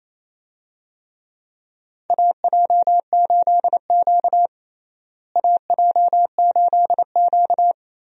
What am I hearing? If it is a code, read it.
Morse code, heard as a keyed tone, about 28 wpm, 700 Hz. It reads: AJ8Q AJ8Q
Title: USA Call Signs - 4 Characters - Repeated 3x - 28wpm